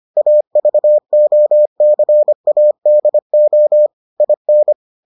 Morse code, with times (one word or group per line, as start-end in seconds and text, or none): 0.2-3.9 AVOCADO
4.2-4.7 IN